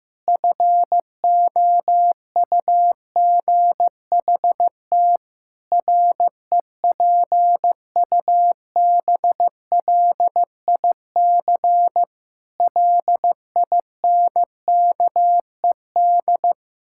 Morse code, heard as a keyed tone, about 15 wpm, 700 Hz. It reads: FOUGHT REPUBLIC LINKED
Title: Pangram Challenge v2 - 15wpm